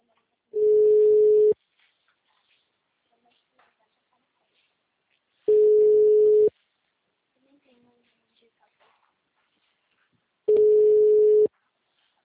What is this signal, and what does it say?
Ringback tone (Europe, 425 Hz), with speech.
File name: out- -2010- - - .601